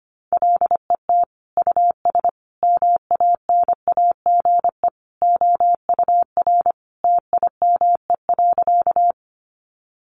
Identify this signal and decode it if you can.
Morse code, heard as a keyed tone, about 25 wpm, 700 Hz. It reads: LET US MANAGE OUR TIME.
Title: Sentences from Top 1400 Words 25wpm